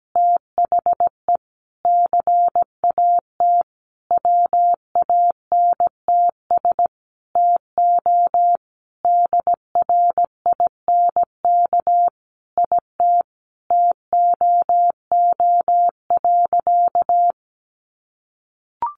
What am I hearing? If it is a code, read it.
Morse code, heard as a keyed tone, about 17 wpm, 700 Hz. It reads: THE CAT WANTS TO DRINK IT TOO.
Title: Additional Easy Phrases 17wpm